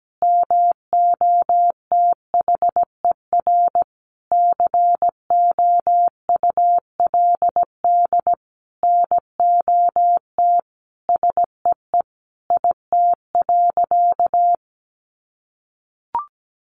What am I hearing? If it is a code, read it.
Morse code, heard as a keyed tone, about 17 wpm, 700 Hz. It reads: MOTHER COULD NOT SEE IT.
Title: Sentences from Top 1000 Words 17wpm